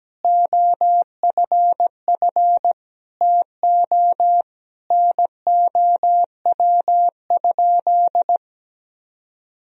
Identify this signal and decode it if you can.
Morse code, heard as a keyed tone, about 17 wpm, 700 Hz. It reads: OFF TO NOW?
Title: Sentences from Top 200 Words and Abbreviations 17wpm